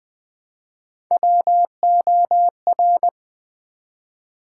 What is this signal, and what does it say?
Morse code, heard as a keyed tone, about 20 wpm, 700 Hz. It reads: WOR
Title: Build A Word - 1 Letter At A Time - 3 to 5-Letter Words - 20wpm